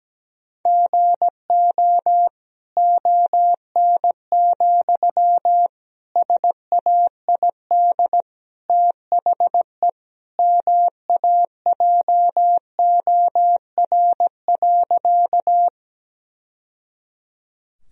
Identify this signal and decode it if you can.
Morse code, heard as a keyed tone, about 17 wpm, 700 Hz. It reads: GO ON, SAID THE MAJOR.
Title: Sentences from Top 1000 Words - Review Part 4 - 17wpm